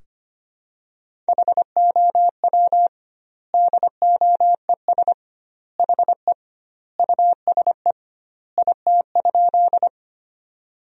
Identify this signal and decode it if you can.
Morse code, heard as a keyed tone, about 25 wpm, 700 Hz. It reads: HOW DOES HE USE IT?
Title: 2x Word-Spacing - Sentences from Top 200 Words - 25wpm